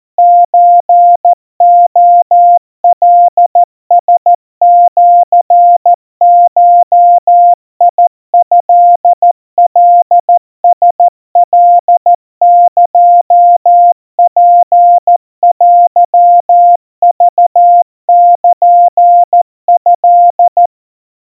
Morse code (Japanese Wabun code), heard as a keyed tone, about 14 wpm, 700 Hz. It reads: ソレカラシゴトカラカエツテクルト
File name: kawa / Morse sample / kamyu 50 cpm 700 Hz Crispy